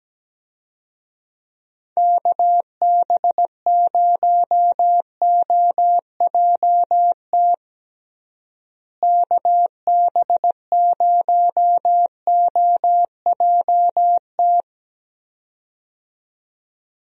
Morse code, heard as a keyed tone, about 17 wpm, 700 Hz. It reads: KB0OJT KB0OJT
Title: USA Call Signs - 6 Characters - Repeated 2x - 17wpm